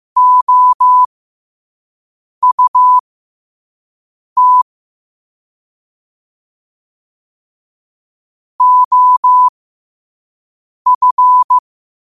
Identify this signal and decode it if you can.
Morse code, heard as a keyed tone, about 15 wpm, spaced as for 5 wpm, 1000 Hz. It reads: OUT OF
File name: qotdfiles / 5wpm